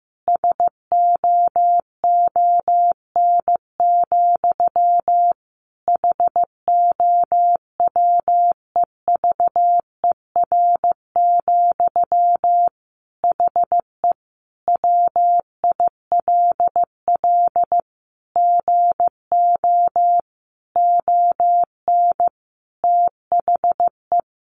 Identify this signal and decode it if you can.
Morse code, heard as a keyed tone, about 15 wpm, 700 Hz. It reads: SOON, HOWEVER, HE WILL GO ON THE
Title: Sentences from Top 1400 Words 15wpm